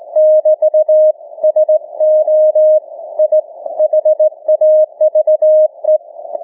Morse code, heard as a keyed tone, about 18 wpm, 620 Hz. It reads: <BT> SO I HAVE
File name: cw / ssn 30M